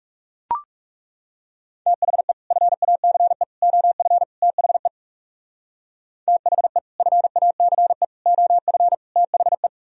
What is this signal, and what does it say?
Morse code, heard as a keyed tone, about 36 wpm, 700 Hz. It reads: THE FACE OF THE THE FACE OF THE
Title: Speed-Racing - Sets of 4 Words 30wpm